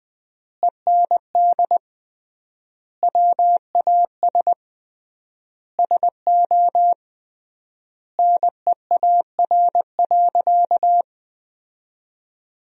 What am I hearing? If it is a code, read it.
Morse code, heard as a keyed tone, about 20 wpm, 700 Hz. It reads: END WAS SO NEAR.
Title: 3x Word-Spacing - Sentences from Top 300 Words - 20wpm